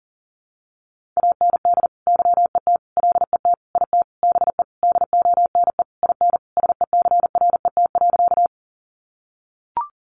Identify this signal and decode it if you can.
Morse code, heard as a keyed tone, about 40 wpm, 700 Hz. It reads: AND YET LET IT BE DONE IN SECRET.
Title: Sentences from Top 1200 Words 40wpm